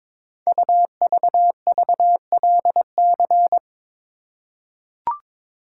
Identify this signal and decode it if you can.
Morse code, heard as a keyed tone, about 22 wpm, 700 Hz. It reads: UVVLC
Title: Code Groups - 5 Letters - 22wpm